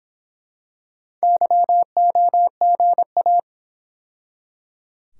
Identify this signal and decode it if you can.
Morse code, heard as a keyed tone, about 26 wpm, 700 Hz. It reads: YOGA